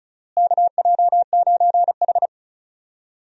Morse code, heard as a keyed tone, about 34 wpm, 700 Hz. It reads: KJ9H
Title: USA Call Signs - 4 Characters - Repeated 2x - 35wpm